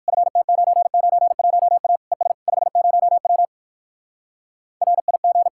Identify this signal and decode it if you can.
Morse code, heard as a keyed tone, about 59 wpm, 700 Hz. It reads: FT991A ES 50W RIG